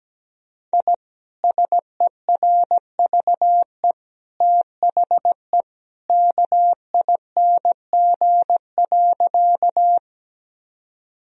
Morse code, heard as a keyed tone, about 17 wpm, 700 Hz. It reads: I SERVE THE KING.
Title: Sentences from Top 1000 Words - Review Part 2 - 17wpm